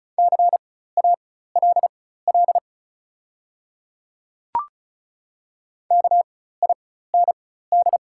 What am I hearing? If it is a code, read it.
Morse code, heard as a keyed tone, about 35 wpm, spaced as for 16 wpm, 700 Hz. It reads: CALL KIND
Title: 4x Character Spacing - Top 500 Words - 35wpm